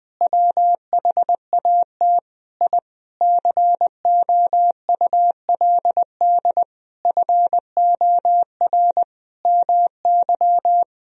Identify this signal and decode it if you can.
Morse code, heard as a keyed tone, about 20 wpm, 700 Hz. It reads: WHAT I COULD FOR MY
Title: Sentences from Top 1000 Words - Review Part 2 - 20wpm